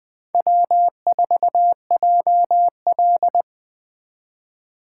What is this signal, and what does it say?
Morse code, heard as a keyed tone, about 20 wpm, 700 Hz. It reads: W4JL
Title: 2023 SOTA Chaser Call Signs - 20wpm